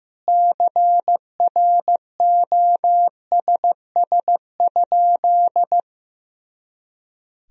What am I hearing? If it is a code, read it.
Morse code, heard as a keyed tone, about 15 wpm, 700 Hz. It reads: CROSS?
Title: Sentences from Top 300 Words and Abbreviations 15wpm